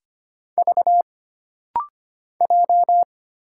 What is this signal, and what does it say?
Morse code, heard as a keyed tone, about 25 wpm, 700 Hz. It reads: V J
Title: Single Letter-Number Alt - Rapid-Fire 25wpm